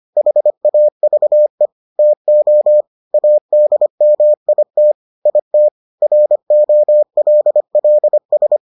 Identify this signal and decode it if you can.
Morse code, heard as a keyed tone, about 25 wpm, 600 Hz. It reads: HAVE TO ADMIT IT ROLLS